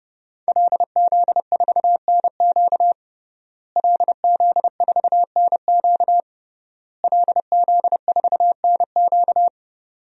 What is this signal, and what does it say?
Morse code, heard as a keyed tone, about 30 wpm, 700 Hz. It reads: LZ4NQ LZ4NQ LZ4NQ
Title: DXCC-100 Call Signs - 5 Characters - Repeated 3x - 30wpm